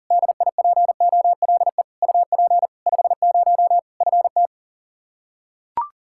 Morse code, heard as a keyed tone, about 40 wpm, 700 Hz. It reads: DIPOLE UP 50 FT
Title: Practice QSO - Exchange 2 - Info - 40wpm